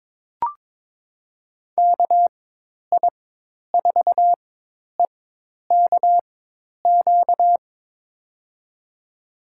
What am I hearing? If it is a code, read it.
Morse code, heard as a keyed tone, about 22 wpm, spaced as for 10 wpm, 700 Hz. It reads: KI4EKQ
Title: USA Call Signs - 4x Character Spacing - 22wpm